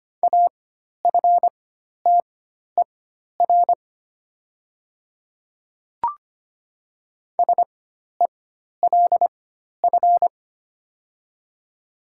Morse code, heard as a keyed tone, about 25 wpm, spaced as for 12 wpm, 700 Hz. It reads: AFTER SELF
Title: 4x Character Spacing - Top 500 Words - 25wpm